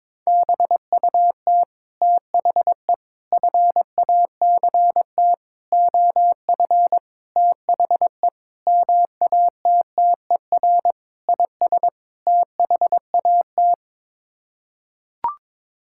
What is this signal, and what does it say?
Morse code, heard as a keyed tone, about 21 wpm, 700 Hz. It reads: BUT THE FACT OF THE MATTER IS THAT
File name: practice / 22wpm / Sets of 8 Words - 22wpm